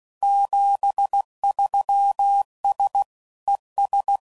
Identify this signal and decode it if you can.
Morse code, heard as a keyed tone, about 16 wpm, 800 Hz. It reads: 73S ES
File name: CW Code Groups @15WPM / QSOs / kc7haa